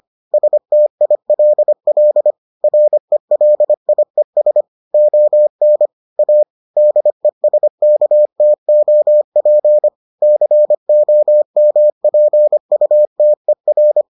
Morse code, heard as a keyed tone, about 25 wpm, 600 Hz. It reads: STILL RELIES ON A DESKTOP COMPUTER